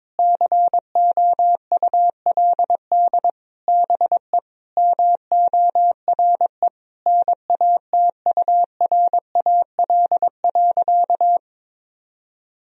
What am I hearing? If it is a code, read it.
Morse code, heard as a keyed tone, about 22 wpm, 700 Hz. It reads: COULD BE MORE NATURAL.
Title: Sentences from Top 1000 Words - Review Part 4 - 22wpm